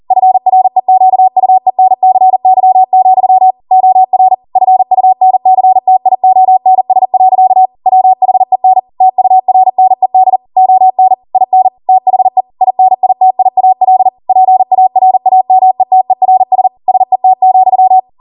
Morse code, heard as a keyed tone, about 40 wpm, 750 Hz. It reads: FREQUENCY, OR FUNCTIONS. WHEN TURNED ON IN THE INITIAL PARAMETERS SET,
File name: Morse / 40WPM